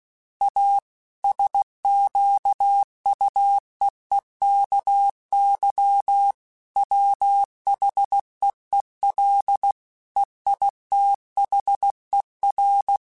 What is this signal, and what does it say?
Morse code, heard as a keyed tone, about 16 wpm, 800 Hz. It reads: A SQUEEKY WHEEL EITHER